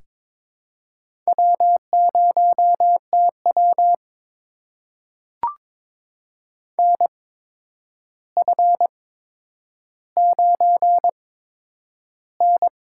Morse code, heard as a keyed tone, about 21 wpm, spaced as for 9 wpm, 700 Hz. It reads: W0TW NF9N
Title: USA Call Signs - 4 Characters - 8x Character Spacing - 22wpm